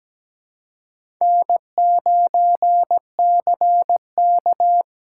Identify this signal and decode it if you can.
Morse code, heard as a keyed tone, about 20 wpm, 700 Hz. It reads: N9CK
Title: Speed Racing - USA Call Signs 17wpm